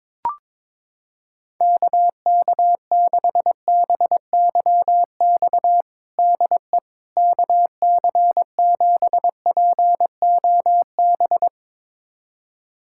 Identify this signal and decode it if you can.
Morse code, heard as a keyed tone, about 22 wpm, 700 Hz. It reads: KK6BYX DE KC7POB
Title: USA Callsign DE Callsign - 6 Characters - 22wpm